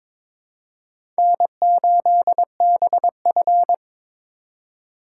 Morse code, heard as a keyed tone, about 22 wpm, 700 Hz. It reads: N8BF
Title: USA Call Signs - 4x Character Spacing - 22wpm